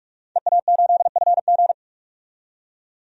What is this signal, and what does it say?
Morse code, heard as a keyed tone, about 46 wpm, 700 Hz. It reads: EA8FG
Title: DXCC-100 Call Signs - 5 Characters - 45wpm